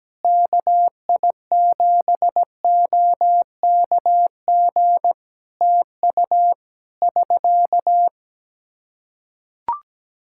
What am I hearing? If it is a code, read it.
Morse code, heard as a keyed tone, about 17 wpm, 700 Hz. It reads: KI7OKG TU <SK>
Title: K1USN SST QSO Practice 17wpm